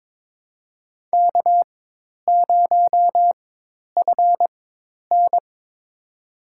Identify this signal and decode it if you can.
Morse code, heard as a keyed tone, about 22 wpm, spaced as for 10 wpm, 700 Hz. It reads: K0FN
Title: USA Call Signs - 4 Characters - 4x Character Spacing - 22wpm